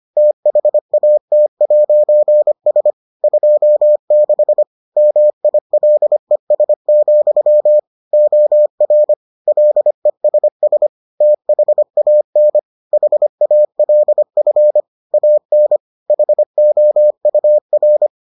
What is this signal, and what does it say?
Morse code, heard as a keyed tone, about 25 wpm, 600 Hz. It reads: THAT'S 26 MILES, OR LESS THAN HALF AN HOUR